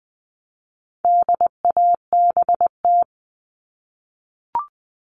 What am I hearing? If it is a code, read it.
Morse code, heard as a keyed tone, about 20 wpm, 700 Hz. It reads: DABT
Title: All Cut Numbers - 4-Digit 20wpm